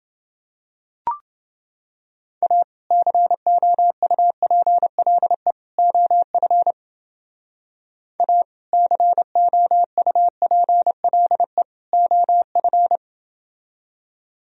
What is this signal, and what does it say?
Morse code, heard as a keyed tone, about 30 wpm, 700 Hz. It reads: A COUPLE OF A COUPLE OF